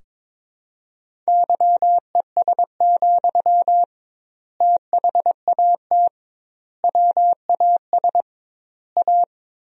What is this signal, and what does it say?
Morse code, heard as a keyed tone, about 22 wpm, 700 Hz. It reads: YES, THAT WAS A
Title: 2x Word-Spacing - Sentences from Top 500 Words - 22wpm